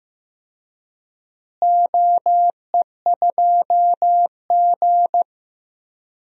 Morse code, heard as a keyed tone, about 15 wpm, 700 Hz. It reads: OE2G
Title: DXCC-100 Call Signs - 4 Characters - Repeated 3x - 15wpm